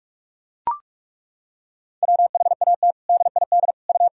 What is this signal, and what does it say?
Morse code, heard as a keyed tone, about 41 wpm, 700 Hz. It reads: WHAT DID U
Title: Sentences from Top 100 Words and Abbreviations 45wpm